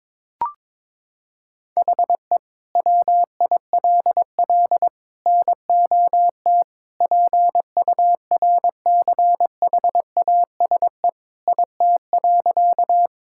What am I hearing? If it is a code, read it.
Morse code, heard as a keyed tone, about 22 wpm, 700 Hz. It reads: HE WILL NOT PURCHASE IT.
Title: Sentences from Top 1000 Words - Review Part 5 - 22wpm